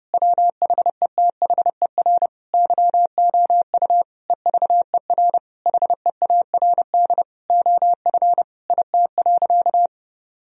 Morse code, heard as a keyed tone, about 32 wpm, 700 Hz. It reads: WHETHER YOU EVER HEARD OF IT.
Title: Sentences from Top 1000 Words - Review Part 3 - 30wpm